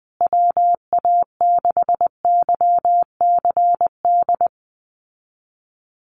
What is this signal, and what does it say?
Morse code, heard as a keyed tone, about 20 wpm, 700 Hz. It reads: WA6YCD
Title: USA Call Signs Repeated 2x 20wpm